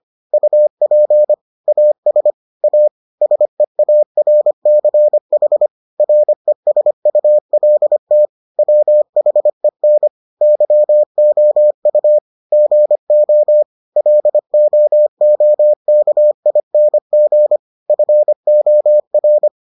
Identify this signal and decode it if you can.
Morse code, heard as a keyed tone, about 25 wpm, 600 Hz. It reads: UP AS A SEARCH RESULT WHEN YOU GO LOOKING FOR